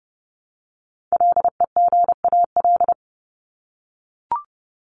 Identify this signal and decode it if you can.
Morse code, heard as a keyed tone, about 30 wpm, 700 Hz. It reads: LEGAL
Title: Top 1001-1250 Words 30wpm